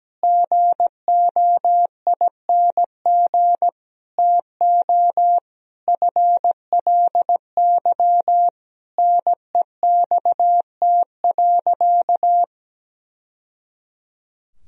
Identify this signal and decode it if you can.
Morse code, heard as a keyed tone, about 17 wpm, 700 Hz. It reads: GOING TO FLY NEXT.
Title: Sentences from Top 1000 Words - Review Part 2 - 17wpm